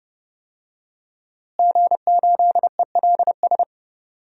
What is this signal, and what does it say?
Morse code, heard as a keyed tone, about 30 wpm, 700 Hz. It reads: G8ELS
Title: DXCC-100 Call Signs - 5 Characters - Repeated 2x - 30wpm